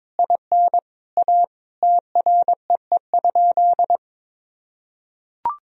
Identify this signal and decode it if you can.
Morse code, heard as a keyed tone, about 22 wpm, 700 Hz. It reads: IN A TREE?
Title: Sentences from Top 1000 Words - Review Part 1 - 22wpm